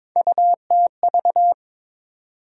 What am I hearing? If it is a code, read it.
Morse code, heard as a keyed tone, about 22 wpm, 700 Hz. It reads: UTV
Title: USA Call Sign Postfixes - 3 Letters 22wpm